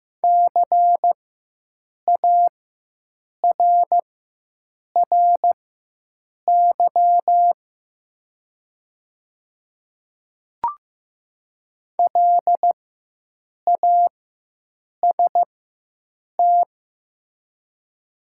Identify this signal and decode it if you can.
Morse code, heard as a keyed tone, about 15 wpm, spaced as for 7 wpm, 700 Hz. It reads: CARRY LAST
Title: 4x Character Spacing - Top 300 Words - 15wpm